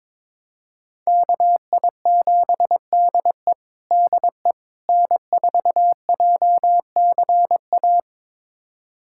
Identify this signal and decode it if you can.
Morse code, heard as a keyed tone, about 22 wpm, 700 Hz. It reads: KI7DE DE N4JCA
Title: USA Callsign DE Callsign - 5 Characters - 22wpm